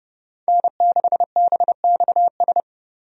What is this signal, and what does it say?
Morse code, heard as a keyed tone, about 30 wpm, 700 Hz. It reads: N6BXS
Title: USA Call Signs - 5 Characters - Repeated 3x - 30wpm